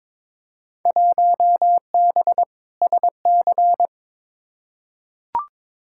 Morse code, heard as a keyed tone, about 22 wpm, 700 Hz. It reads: 1B SC
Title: Field Day Exchanges - 22wpm